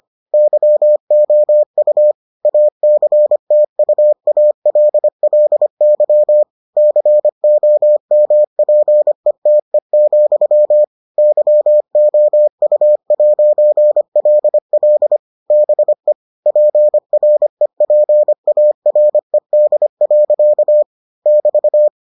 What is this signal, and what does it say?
Morse code, heard as a keyed tone, about 25 wpm, 600 Hz. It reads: YOU ACTUALLY COMPETE, YOU'LL BE PREPARED. <BT>